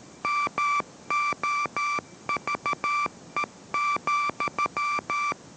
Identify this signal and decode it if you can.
Morse code, heard as a keyed tone, about 13 wpm, 1170 Hz, with noise. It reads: MOVE,